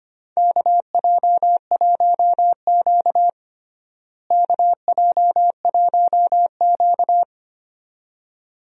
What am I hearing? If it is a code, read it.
Morse code, heard as a keyed tone, about 25 wpm, 700 Hz. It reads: KJ1Q KJ1Q
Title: USA Call Signs - 4 Characters - Repeated 2x - 25wpm